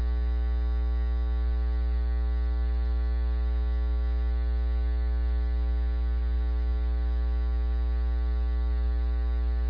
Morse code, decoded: none

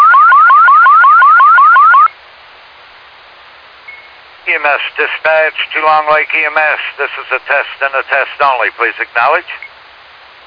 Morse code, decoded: none

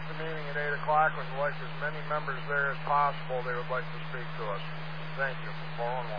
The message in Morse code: none

